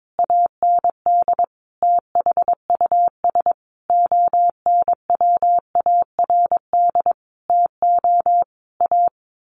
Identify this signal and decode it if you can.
Morse code, heard as a keyed tone, about 22 wpm, 700 Hz. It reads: AND THUS ONWARD TO A
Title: A Sound Of Thunder 22wpm